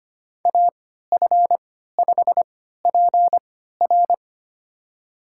Morse code, heard as a keyed tone, about 25 wpm, spaced as for 14 wpm, 700 Hz. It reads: AF5PR
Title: USA Call Signs - 5 Characters - 3x Character Spacing - 25wpm